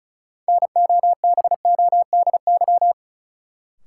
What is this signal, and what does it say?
Morse code, heard as a keyed tone, about 36 wpm, 700 Hz. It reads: NOBODY
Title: Sentences from Top 1600 Words - New Word Drill Alt - 35wpm